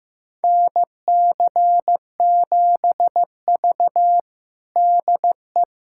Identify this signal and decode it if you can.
Morse code, heard as a keyed tone, about 15 wpm, 700 Hz. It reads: NC7V DE
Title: USA Callsign DE Callsign - 4 Characters - 15wpm